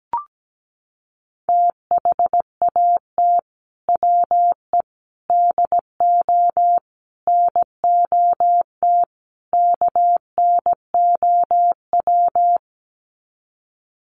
Morse code, heard as a keyed tone, about 17 wpm, 700 Hz. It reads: THAT WE DO NOT KNOW
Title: Sets of 5 Words Encore2 17wpm